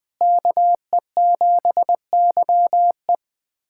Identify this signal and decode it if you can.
Morse code, heard as a keyed tone, about 20 wpm, 700 Hz. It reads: KE7YE
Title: USA Callsign DE Callsign - 5 Characters - 20wpm